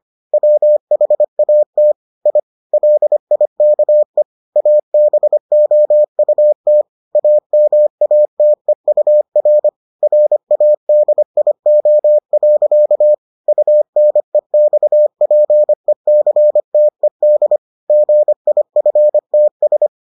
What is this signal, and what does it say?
Morse code, heard as a keyed tone, about 25 wpm, 600 Hz. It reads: WHAT I LIKE ABOUT AMATEUR RADIO. UNEXPECTED GIFTS